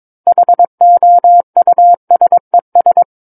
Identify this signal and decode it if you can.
Morse code, heard as a keyed tone, about 22 wpm, 700 Hz. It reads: HOUSES
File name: Book One / 13-Chapter 13 - 22 WPM